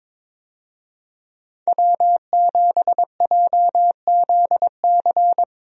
Morse code, heard as a keyed tone, about 22 wpm, 700 Hz. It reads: W7JZC
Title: USA Call Signs - 5 Characters - Repeated 2x - 22wpm